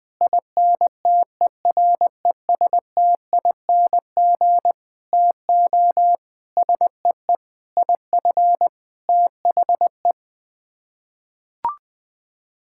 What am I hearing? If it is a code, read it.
Morse code, heard as a keyed tone, about 20 wpm, 700 Hz. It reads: INTERESTING TO SEE IF THE
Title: Sets of 8 Words - 20wpm